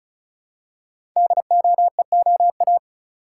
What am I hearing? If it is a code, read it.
Morse code, heard as a keyed tone, about 34 wpm, 700 Hz. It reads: DOEOA
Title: Code Groups - 5 Letters - 35wpm